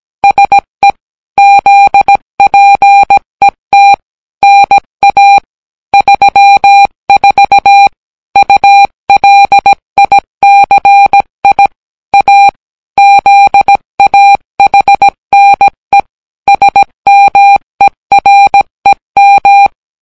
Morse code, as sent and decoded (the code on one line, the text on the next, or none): ... / --.. .--. - / -. .- / ...-- ....- / ..- .-.. .. -.-. .. / .- / --.. .- .... -. / ... -- . .-. --
SE ZPET NA 34 ULICI A ZAHNE SMEREM